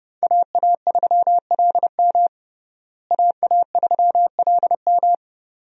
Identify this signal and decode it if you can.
Morse code, heard as a keyed tone, about 30 wpm, 700 Hz. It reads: AA3LM AA3LM
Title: USA Call Signs - 5 Characters - Repeated 2x - 30wpm